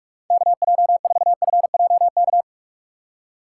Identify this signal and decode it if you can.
Morse code, heard as a keyed tone, about 44 wpm, 700 Hz. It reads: KJ4FJK
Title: USA Call Signs - 6 Characters - 3x Character Spacing - 45wpm